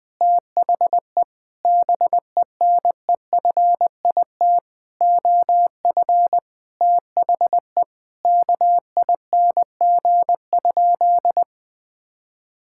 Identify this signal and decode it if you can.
Morse code, heard as a keyed tone, about 20 wpm, 700 Hz. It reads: THE BENEFIT OF THE KING?
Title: Sentences from Top 1000 Words - Review Part 6 - 20wpm